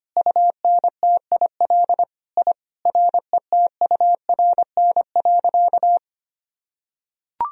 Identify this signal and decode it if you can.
Morse code, heard as a keyed tone, about 25 wpm, 700 Hz. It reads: UNTIL I RETURN.
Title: Sentences from Top 1000 Words - Review Part 1 - 25wpm